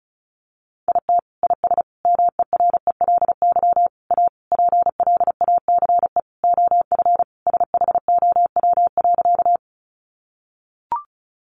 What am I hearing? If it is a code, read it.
Morse code, heard as a keyed tone, about 34 wpm, 700 Hz. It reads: IT IS MERELY A PLACE OF SHOW.
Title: Sentences from Top 1400 Words 35wpm